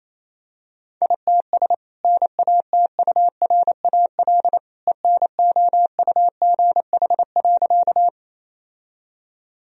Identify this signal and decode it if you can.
Morse code, heard as a keyed tone, about 28 wpm, 700 Hz. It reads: ITS NATURAL ENOUGH.
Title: Sentences from Top 1000 Words - Review Part 4 - 28wpm